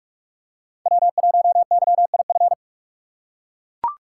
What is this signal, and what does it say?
Morse code, heard as a keyed tone, about 44 wpm, 690 Hz. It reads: W1YIF
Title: USA Call Signs - 5 Characters - 6x Character Spacing - 45wpm